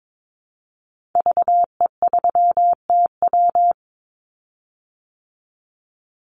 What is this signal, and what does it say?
Morse code, heard as a keyed tone, about 22 wpm, 700 Hz. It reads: VE3TW
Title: Call Signs Repeated 3x - 22wpm